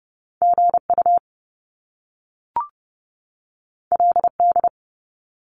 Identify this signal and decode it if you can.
Morse code, heard as a keyed tone, about 30 wpm, 700 Hz. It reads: GU LD
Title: Top Two Letter Combinations in Words 30wpm